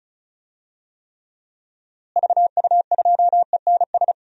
Morse code, heard as a keyed tone, about 34 wpm, 700 Hz. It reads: VU2ENS
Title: DXCC-100 Call Signs - 6 Characters - Repeated 3x - 35wpm